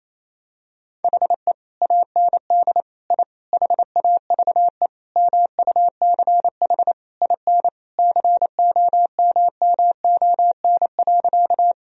Morse code, heard as a keyed tone, about 27 wpm, 700 Hz. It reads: HE AND I HAVE MUCH IN COMMON.
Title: Sentences from Top 1000 Words - Review Part 3 - 28wpm